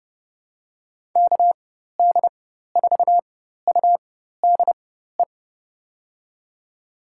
Morse code, heard as a keyed tone, about 30 wpm, spaced as for 14 wpm, 700 Hz. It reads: KD4UDE
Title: USA Call Signs - 6 Characters - 4x Character Spacing - 30wpm